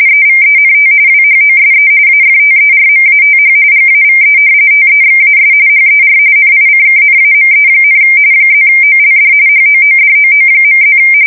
RTTY, about 45.45 baud. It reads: EVENT WHOSE AFTERMATH GAVE TEXAS HER NICKNAME, +OLD HOODOO.+
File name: fsk170 / battleships